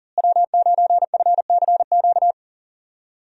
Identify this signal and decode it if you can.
Morse code, heard as a keyed tone, about 40 wpm, 700 Hz. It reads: W9FCQ
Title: USA Call Signs - 5 Characters - Repeated 3x - 40wpm